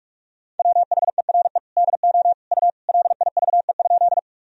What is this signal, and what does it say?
Morse code, heard as a keyed tone, about 46 wpm, 700 Hz. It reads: WHERE DO U LIVE?
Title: Sentences from Top 200 Words and Abbreviations 45wpm